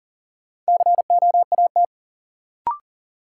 Morse code, heard as a keyed tone, about 40 wpm, 700 Hz. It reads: COAT
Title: Sentences from Top 1600 Words - New Word Drill - 40wpm